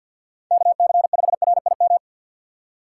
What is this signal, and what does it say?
Morse code, heard as a keyed tone, about 52 wpm, 700 Hz. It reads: KC5RIM
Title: USA Call Signs - 4x Character Spacing - 50wpm